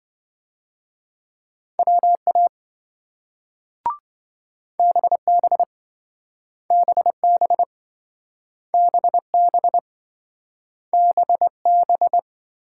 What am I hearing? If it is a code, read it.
Morse code, heard as a keyed tone, about 26 wpm, 700 Hz. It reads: WA BB BB BB BB